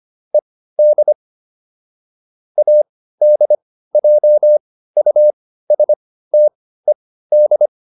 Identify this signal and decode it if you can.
Morse code, heard as a keyed tone, about 25 wpm, spaced as for 15 wpm, 600 Hz. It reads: ED ADJUSTED